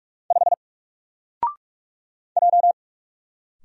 Morse code, heard as a keyed tone, about 46 wpm, 700 Hz. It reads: 5 J
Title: Single Character (Letter:Number:Punctuation:Prosign) 45wpm